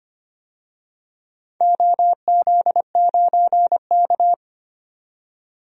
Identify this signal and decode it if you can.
Morse code, heard as a keyed tone, about 25 wpm, 700 Hz. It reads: OZ9K
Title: DXCC-100 Call Signs - 4 Characters - 25wpm